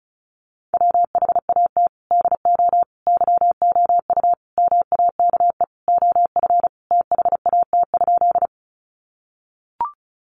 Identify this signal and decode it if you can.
Morse code, heard as a keyed tone, about 34 wpm, 700 Hz. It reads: WHAT DO YOU MAKE OF THAT?
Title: Sentences from Top 100 Words 35wpm